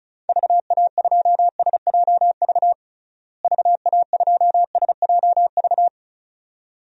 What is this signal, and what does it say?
Morse code, heard as a keyed tone, about 36 wpm, 700 Hz. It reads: VA2SJV VA2SJV
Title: DXCC-100 Call Signs - 6 Characters - Repeated 2x - 35wpm